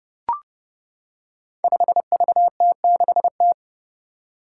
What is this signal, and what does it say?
Morse code, heard as a keyed tone, about 30 wpm, 700 Hz. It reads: 5VT6T
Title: Five Challenging Letter-Number Combinations 30wpm